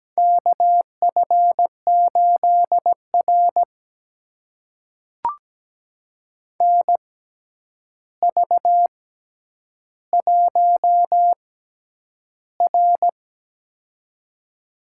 Morse code, heard as a keyed tone, about 17 wpm, spaced as for 9 wpm, 700 Hz. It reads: KF8R NV1R
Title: USA Call Signs - 4 Characters - 6x Character Spacing - 17wpm